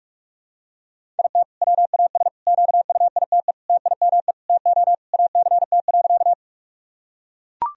Morse code, heard as a keyed tone, about 44 wpm, 700 Hz. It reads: IT WAS QUITE TIME TO ACT.
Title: Sentences from Top 1000 Words - Review Part 1 - 45wpm